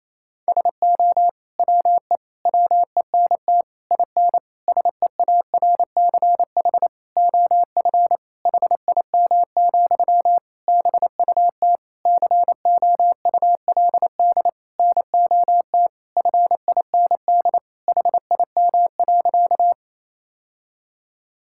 Morse code, heard as a keyed tone, about 28 wpm, 700 Hz. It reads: SO WE WENT IN SEARCH OF HIM, BUT COULD NOT FIND HIM.
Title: Sentences from Top 1000 Words - Review Part 5 - 28wpm